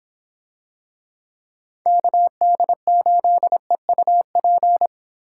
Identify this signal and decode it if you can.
Morse code, heard as a keyed tone, about 26 wpm, 700 Hz. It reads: KD8EUP